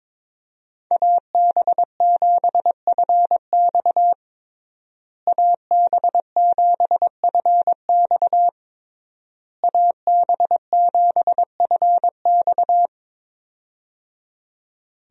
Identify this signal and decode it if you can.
Morse code, heard as a keyed tone, about 22 wpm, 700 Hz. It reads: AB7FX AB7FX AB7FX
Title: USA Call Signs - 5 Characters - Repeated 3x - 22wpm